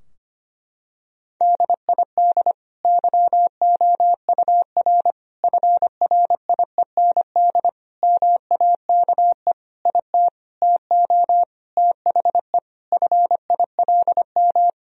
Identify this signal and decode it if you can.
Morse code, heard as a keyed tone, about 25 wpm, 700 Hz. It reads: DID YOUR FRIEND MAKE IT TO THE FILM
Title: Sentences from Top 1000 Words - Review Part 3 - 25wpm